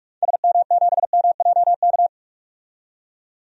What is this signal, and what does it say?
Morse code, heard as a keyed tone, about 46 wpm, 700 Hz. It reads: SM7MJK
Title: DXCC-100 Call Signs - 6 Characters - 45wpm